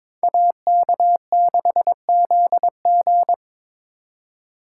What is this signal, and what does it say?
Morse code, heard as a keyed tone, about 26 wpm, 700 Hz. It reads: AK6ZG
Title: Speed Racing - USA 2x2 Call Signs 22wpm